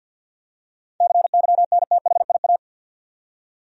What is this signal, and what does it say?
Morse code, heard as a keyed tone, about 51 wpm, 700 Hz. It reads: CYNTHIA
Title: USA Common Names - 50wpm